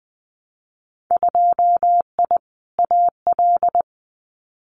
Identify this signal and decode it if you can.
Morse code, heard as a keyed tone, about 19 wpm, 700 Hz. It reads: 2I AL
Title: Winter Field Day Exchanges 20wpm